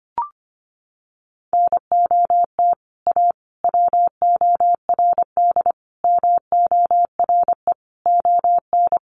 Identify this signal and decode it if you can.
Morse code, heard as a keyed tone, about 25 wpm, 700 Hz. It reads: NOT A WORD MORE ON
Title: Sentences from Top 800 Words 25wpm